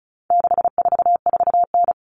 Morse code, heard as a keyed tone, about 34 wpm, 700 Hz. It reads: 644N
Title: All Cut Numbers - 4-Digit 35wpm